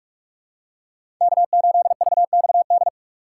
Morse code, heard as a keyed tone, about 43 wpm, 700 Hz. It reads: K8VXD
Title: USA Call Signs - 5 Characters - 3x Character Spacing - 45wpm